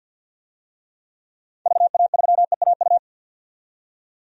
Morse code, heard as a keyed tone, about 46 wpm, 700 Hz. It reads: VA3EAU
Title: DXCC-100 Call Signs - 6 Characters - 50wpm